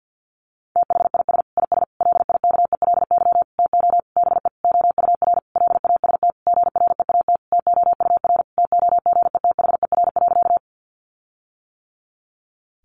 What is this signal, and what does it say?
Morse code, heard as a keyed tone, about 50 wpm, 700 Hz. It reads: THIS IS LIKELY TO BE OUR LAST GREAT TOUR TOGETHER.
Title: Sentences from Top 1400 Words 50wpm